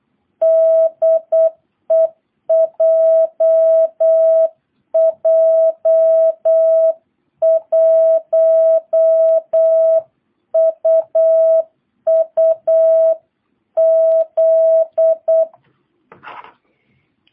Morse code, heard as a keyed tone, about 8 wpm, 660 Hz. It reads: DE JJ1UUZ